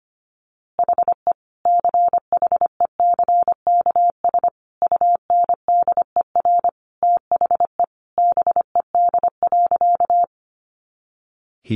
Morse code, heard as a keyed tone, about 25 wpm, 700 Hz. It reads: HE CHECKS UNDER THE BED.
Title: Easy Phrases 25wpm